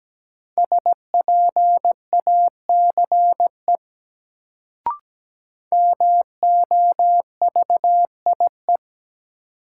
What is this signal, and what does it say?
Morse code, heard as a keyed tone, about 17 wpm, 700 Hz. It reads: SPACE MOVIE
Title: Sentences from Top 1000 Words - Review Part 5 - New Word Drill - 17wpm